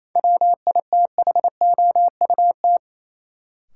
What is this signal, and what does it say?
Morse code, heard as a keyed tone, about 28 wpm, 700 Hz. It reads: WITHOUT
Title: Sentences from Top 600 Words - New Word Drill - 28wpm